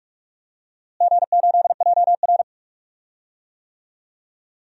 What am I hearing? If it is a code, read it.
Morse code, heard as a keyed tone, about 46 wpm, 700 Hz. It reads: G8JR
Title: DXCC-100 Call Signs - 4 Characters - Repeated 2x - 45wpm